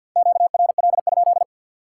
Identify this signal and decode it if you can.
Morse code, heard as a keyed tone, about 53 wpm, 700 Hz. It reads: QRL?
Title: Top 90 QSO Elements 50wpm